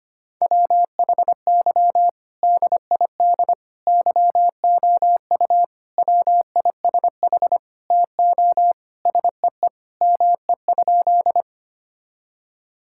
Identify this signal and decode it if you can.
Morse code, heard as a keyed tone, about 25 wpm, 700 Hz. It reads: WHY DID YOU WISH TO SEE ME?
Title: Sentences from Top 1600 Words 25wpm